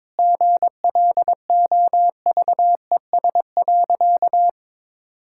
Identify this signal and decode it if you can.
Morse code, heard as a keyed tone, about 22 wpm, 690 Hz. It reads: GLOVES.
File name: practice / 22wpm / Easy Sentences with Hints - 22wpm